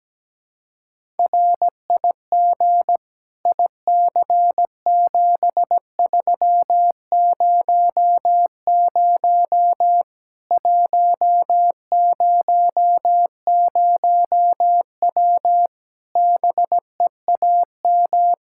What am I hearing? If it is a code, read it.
Morse code, heard as a keyed tone, about 17 wpm, 700 Hz. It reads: RIG IC7300 100W BEAM
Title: Practice QSO - Exchange 2 - Info - 17wpm